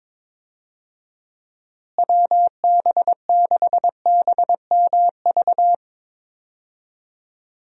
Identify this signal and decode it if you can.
Morse code, heard as a keyed tone, about 22 wpm, 700 Hz. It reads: WB6BMV
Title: USA Call Signs - 6 Characters - Repeated 2x - 22wpm